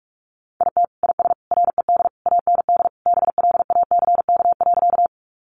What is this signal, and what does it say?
Morse code, heard as a keyed tone, about 44 wpm, 700 Hz. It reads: IT IS RED AND BLACK.
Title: Additional Easy Phrases 45wpm